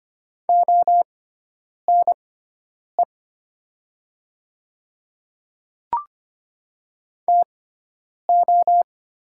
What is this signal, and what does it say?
Morse code, heard as a keyed tone, about 25 wpm, spaced as for 9 wpm, 700 Hz. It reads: ONE TO
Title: 6x Character Spacing - Top 100 Words - 25wpm